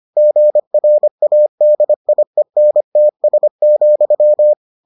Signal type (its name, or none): Morse code